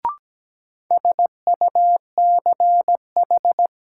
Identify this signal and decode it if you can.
Morse code, heard as a keyed tone, about 17 wpm, 700 Hz. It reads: SUCH
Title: Sentences from Top 200 Words - New Word Drill - 17wpm